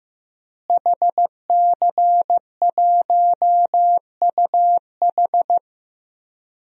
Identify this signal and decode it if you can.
Morse code, heard as a keyed tone, about 15 wpm, 700 Hz. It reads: HC1UH
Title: DXCC-100 Call Signs - 5 Characters - Repeated 3x - 15wpm